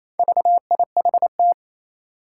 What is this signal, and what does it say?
Morse code, heard as a keyed tone, about 29 wpm, 700 Hz. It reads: VIHT
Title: Four Challenging Letter-Number Combinations 28wpm